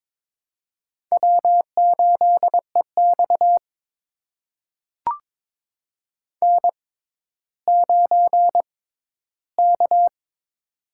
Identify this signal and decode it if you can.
Morse code, heard as a keyed tone, about 22 wpm, spaced as for 13 wpm, 700 Hz. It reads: W8EX N9K
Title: USA Call Signs - 6x Character Spacing - 22wpm